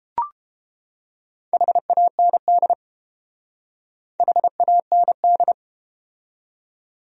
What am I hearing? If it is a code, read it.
Morse code, heard as a keyed tone, about 30 wpm, 700 Hz. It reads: HAND HAND